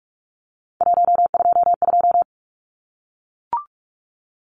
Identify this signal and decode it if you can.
Morse code, heard as a keyed tone, about 44 wpm, 700 Hz. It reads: 122
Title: Numbers - 3-Digit 45wpm